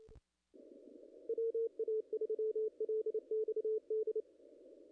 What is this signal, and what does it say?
Morse code, heard as a keyed tone, about 28 wpm, 440 Hz. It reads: WA3LXD
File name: contests / ss-cw-k5tr-op-n5rz -0000